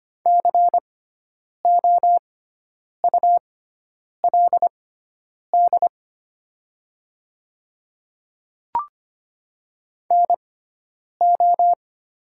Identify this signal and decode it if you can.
Morse code, heard as a keyed tone, about 25 wpm, spaced as for 9 wpm, 700 Hz. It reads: COULD NO
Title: 6x Character Spacing - Top 100 Words - 25wpm